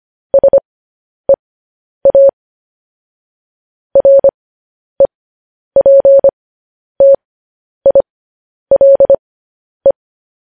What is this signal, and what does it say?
Morse code, heard as a keyed tone, about 25 wpm, spaced as for 10 wpm, 550 Hz. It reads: SEA REPTILE
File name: podcast / News headlines-1x-25x10